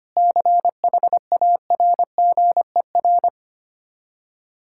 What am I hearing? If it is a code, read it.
Morse code, heard as a keyed tone, about 25 wpm, 700 Hz. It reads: CHARGER
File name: practice / 25wpm / Difficult Words with Hints - 25wpm